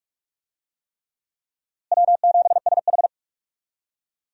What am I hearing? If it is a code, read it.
Morse code, heard as a keyed tone, about 45 wpm, 700 Hz. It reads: W7SH
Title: USA Call Signs - 4 Characters - Repeated 3x - 45wpm